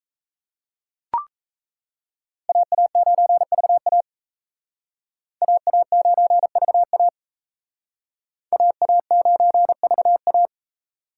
Morse code, heard as a keyed tone, about 35 wpm, 700 Hz. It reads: AA9VA AA9VA AA9VA